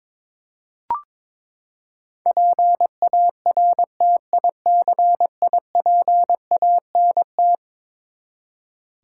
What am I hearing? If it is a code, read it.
Morse code, heard as a keyed tone, about 21 wpm, 700 Hz. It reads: PARTICIPANT
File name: practice / 22wpm / Eleven Letter Words - 22wpm